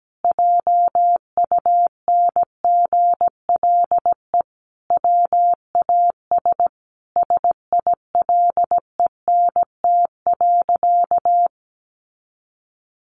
Morse code, heard as a keyed tone, about 17 wpm, 700 Hz. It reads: JUNGLE WAS SILENT.
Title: A Sound Of Thunder 17wpm